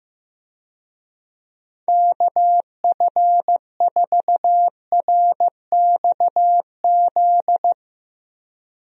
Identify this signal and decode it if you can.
Morse code, heard as a keyed tone, about 16 wpm, 700 Hz. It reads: KF4RXZ
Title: USA Call Signs - 6 Characters - Repeated 3x - 15wpm